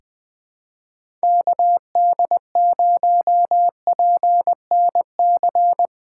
Morse code, heard as a keyed tone, about 20 wpm, 700 Hz. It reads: KD0PNC
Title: USA Callsign DE Callsign - 6 Characters - 20wpm